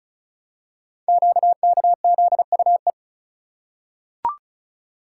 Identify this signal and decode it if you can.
Morse code, heard as a keyed tone, about 35 wpm, 690 Hz. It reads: QKZUE
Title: Code Groups - 5 Letters - 35wpm